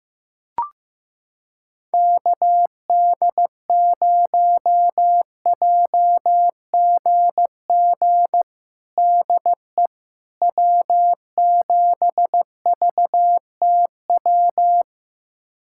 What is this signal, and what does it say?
Morse code, heard as a keyed tone, about 15 wpm, 700 Hz. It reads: KD0JGG DE W7VTW
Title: USA Callsign DE Callsign 15wpm